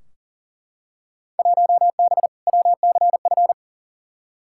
Morse code, heard as a keyed tone, about 40 wpm, 700 Hz. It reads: 1B WCF
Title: Field Day Exchanges Encore - 40wpm